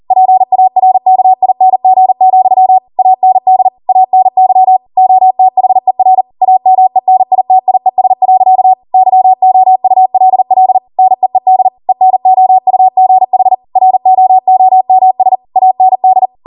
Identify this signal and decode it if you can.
Morse code, heard as a keyed tone, about 40 wpm, 750 Hz. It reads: PARKING, AND ANY OTHER AMENITIES. YOULL NEED ENOUGH ROOMS AND